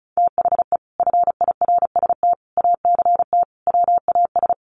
Morse code, heard as a keyed tone, about 34 wpm, 700 Hz. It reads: THE FIRST ACT WAS
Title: Sentences from Top 200 Words 35wpm